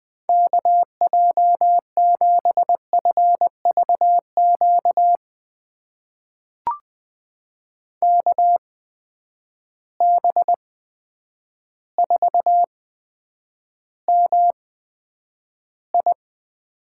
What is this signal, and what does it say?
Morse code, heard as a keyed tone, about 20 wpm, spaced as for 9 wpm, 700 Hz. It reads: KJ7FVQ KB4MI
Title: USA Call Signs - 6 Characters - 8x Character Spacing - 20wpm